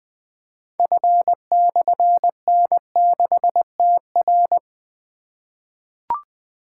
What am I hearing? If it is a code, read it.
Morse code, heard as a keyed tone, about 20 wpm, 700 Hz. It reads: F/N6TR
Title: Difficult Call Signs 20wpm